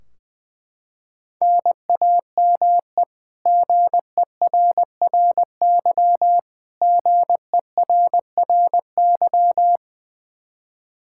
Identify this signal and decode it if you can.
Morse code, heard as a keyed tone, about 21 wpm, 700 Hz. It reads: NAME GERRY GERRY
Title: Practice QSO - Exchange 1 - Beginning - 20wpm